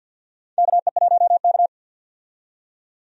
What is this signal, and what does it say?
Morse code, heard as a keyed tone, about 49 wpm, 700 Hz. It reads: KE1K
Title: USA Call Signs - 4 Characters - 50wpm